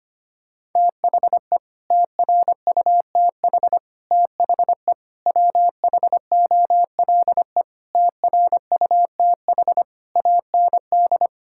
Morse code, heard as a keyed tone, about 25 wpm, 700 Hz. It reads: THE TRUTH THE WHOLE TRUTH AND
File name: Sets of 9 Words - 25wpm